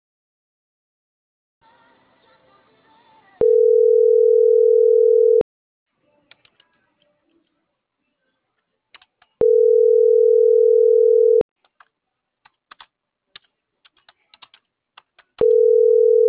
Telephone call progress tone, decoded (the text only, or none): ringback tone